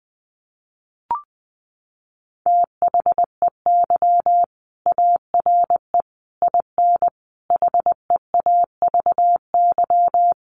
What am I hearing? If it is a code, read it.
Morse code, heard as a keyed tone, about 20 wpm, 700 Hz. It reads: THEY ARE IN HEAVY
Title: Sentences from Top 1300 Words 20wpm